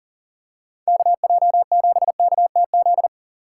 Morse code, heard as a keyed tone, about 40 wpm, 700 Hz. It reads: KJ7KTZ
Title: USA Call Signs - 6 Characters - 3x Character Spacing - 40wpm